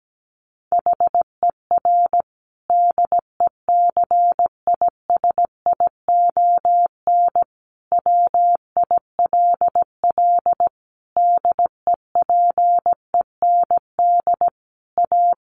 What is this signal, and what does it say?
Morse code, heard as a keyed tone, about 17 wpm, 700 Hz. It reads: HER DECISION WILL DEPEND A
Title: Sentences from Top 1300 Words 17wpm